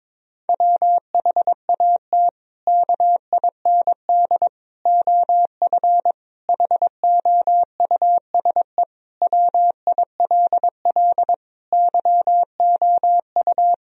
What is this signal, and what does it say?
Morse code, heard as a keyed tone, about 22 wpm, 700 Hz. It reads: WHAT KIND OF HOUSE WILL YOU
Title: Sentences from Top 1000 Words - Review Part 1 - 22wpm